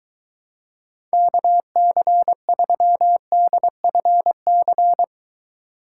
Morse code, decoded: KC3DFC

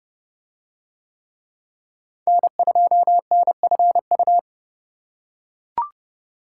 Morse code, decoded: N2NFU

